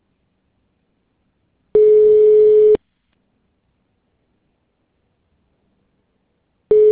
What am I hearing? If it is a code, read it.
Ringback tone (Europe, 425 Hz).